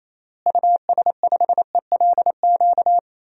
Morse code, heard as a keyed tone, about 28 wpm, 700 Hz. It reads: US5ELQ
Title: DXCC-100 Call Signs - 6 Characters - Repeated 2x - 28wpm